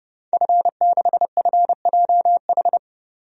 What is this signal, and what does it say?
Morse code, heard as a keyed tone, about 32 wpm, 700 Hz. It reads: F6FJH
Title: DXCC-100 Call Signs - 5 Characters - 30wpm